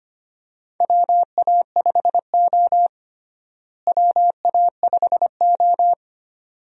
Morse code, heard as a keyed tone, about 25 wpm, 700 Hz. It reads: WA5O WA5O
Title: USA Call Signs - 4 Characters - Repeated 2x - 25wpm